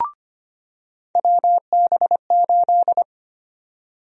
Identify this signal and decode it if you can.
Morse code, heard as a keyed tone, about 25 wpm, 700 Hz. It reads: WB8
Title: USA Call Sign Prefixes - 25wpm